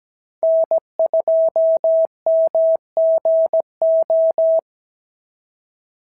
Morse code, heard as a keyed tone, about 16 wpm, 650 Hz. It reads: N2MGO
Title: SOTA QSO Practice 17wpm